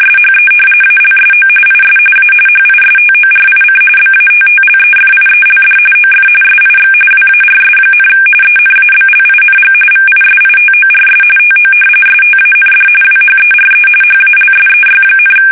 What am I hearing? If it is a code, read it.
RTTY, about 45.45 baud. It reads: SHE ALSO IS A HISTORIC BATTLEFIELD OF SIX WARS, ENABLING VISITORS TO STAND WHERE HALSEY,
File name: fsk850 / battleshipnj